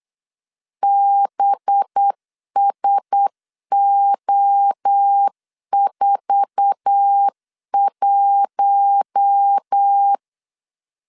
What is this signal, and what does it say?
Morse code, heard as a keyed tone, about 8 wpm, 800 Hz. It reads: BSO41
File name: LEZIONI / 8 YJ45 / 8 YJ4 8wpm 3246